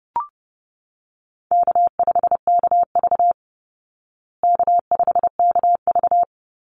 Morse code, heard as a keyed tone, about 30 wpm, 700 Hz. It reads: K5KV K5KV
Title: Easy Call Signs Repeated 2x - 30wpm